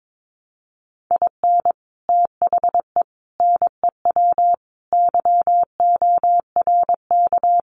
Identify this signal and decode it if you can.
Morse code, heard as a keyed tone, about 22 wpm, 700 Hz. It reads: IN THE NEW YORK
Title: Sets of 5 Words 22wpm